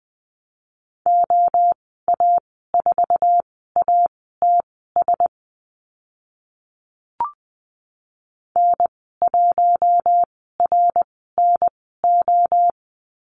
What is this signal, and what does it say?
Morse code, heard as a keyed tone, about 20 wpm, spaced as for 15 wpm, 700 Hz. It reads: OA4ATS N1RNO
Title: 076 Lesson 13g - Letter L - Call Signs 20wpm